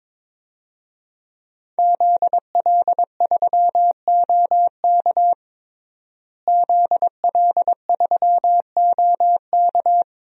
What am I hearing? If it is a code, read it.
Morse code, heard as a keyed tone, about 22 wpm, 700 Hz. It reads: ZL3OK ZL3OK
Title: DXCC-100 Call Signs - 5 Characters - Repeated 2x - 22wpm